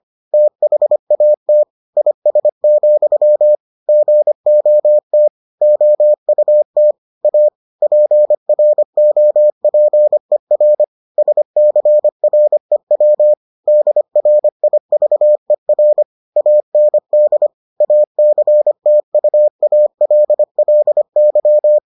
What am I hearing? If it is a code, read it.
Morse code, heard as a keyed tone, about 25 wpm, 600 Hz. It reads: THAT IS, GOT OUT A PROPER SCREW DRIVER AND ACTUALLY